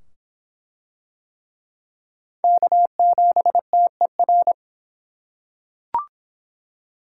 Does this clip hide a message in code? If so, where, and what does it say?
Yes: Morse code from 2.4 to 4.5 s, heard as a keyed tone, about 25 wpm, 700 Hz. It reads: K7TER